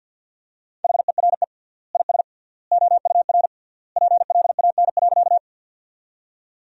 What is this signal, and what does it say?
Morse code, heard as a keyed tone, about 52 wpm, 700 Hz. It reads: HERE IS OUR PLAN.
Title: 3x Word-Spacing - Sentences from Top 500 Words - 50wpm